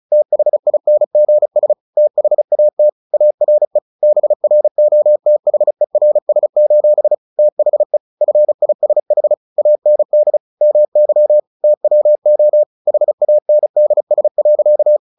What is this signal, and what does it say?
Morse code, heard as a keyed tone, about 34 wpm, 600 Hz. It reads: THINGS THAT ARE BROTHERS: THE FISH AND MY TWO HANDS.
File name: the old man and the sea / the old man and the sea 1 35wpm